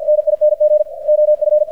Morse code, heard as a keyed tone, about 50 wpm, 600 Hz. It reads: NING OF
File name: archived / CRSnet